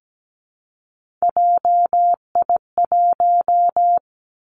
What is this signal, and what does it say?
Morse code, heard as a keyed tone, about 17 wpm, 700 Hz. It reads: JI1
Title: Three Challenging Letter-Number Combinations 17wpm